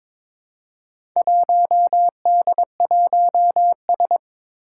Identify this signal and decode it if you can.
Morse code, heard as a keyed tone, about 22 wpm, 700 Hz. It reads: 1D1S
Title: Four Challenging Letter-Number Combinations 22wpm